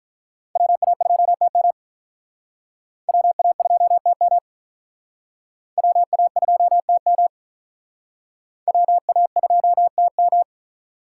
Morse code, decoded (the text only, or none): WA2TM WA2TM WA2TM WA2TM